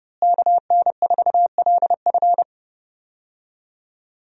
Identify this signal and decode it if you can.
Morse code, heard as a keyed tone, about 30 wpm, 700 Hz. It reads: KN4LF
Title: USA Call Signs - 5 Characters - Repeated 2x - 30wpm